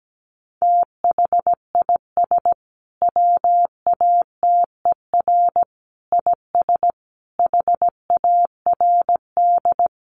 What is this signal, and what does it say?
Morse code, heard as a keyed tone, about 17 wpm, 700 Hz. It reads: THIS WATER IS HARD
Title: Sentences from Top 1300 Words 17wpm